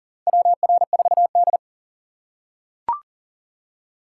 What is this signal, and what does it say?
Morse code, heard as a keyed tone, about 40 wpm, 700 Hz. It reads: WR4D
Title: USA Call Signs - 4 Characters - 6x Character Spacing - 40wpm